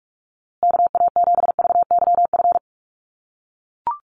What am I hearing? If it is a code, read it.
Morse code, heard as a keyed tone, about 44 wpm, 700 Hz. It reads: KA7VYF
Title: USA Call Signs Repeated 3x 45wpm